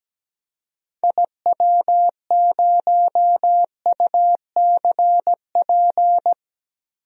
Morse code, heard as a keyed tone, about 17 wpm, 700 Hz. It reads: IW0UCP
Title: DXCC-100 Call Signs - 6 Characters - Repeated 3x - 17wpm